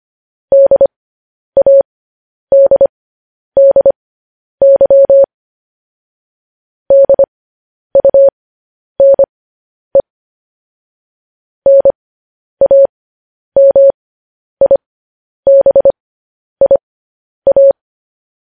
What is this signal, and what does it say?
Morse code, heard as a keyed tone, about 25 wpm, spaced as for 10 wpm, 550 Hz. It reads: DADDY DUNE NAMIBIA
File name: podcast / News headlines-1x-25x10